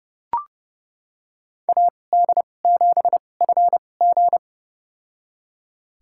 Morse code, heard as a keyed tone, about 32 wpm, spaced as for 22 wpm, 700 Hz. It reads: AD7FG
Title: USA Call Signs - 5 Characters - 2x Character Spacing - 30wpm